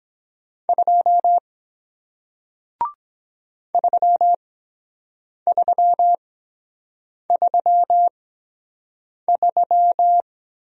Morse code, heard as a keyed tone, about 23 wpm, spaced as for 7 wpm, 700 Hz. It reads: 2 3333